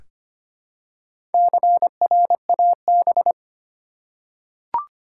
Morse code, heard as a keyed tone, about 25 wpm, 700 Hz. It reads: CRAB
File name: practice / 25wpm / Easy Words with Hints - 25wpm